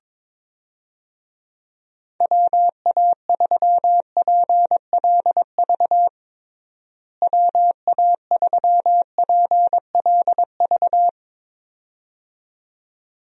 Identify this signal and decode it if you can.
Morse code, heard as a keyed tone, about 21 wpm, 700 Hz. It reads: WA3PLV WA3PLV
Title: USA Call Signs - 6 Characters - Repeated 2x - 22wpm